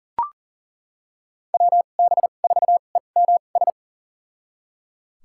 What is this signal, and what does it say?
Morse code, heard as a keyed tone, about 37 wpm, spaced as for 28 wpm, 700 Hz. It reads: WB4EMS
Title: USA Call Signs - 2x Character Spacing - 40wpm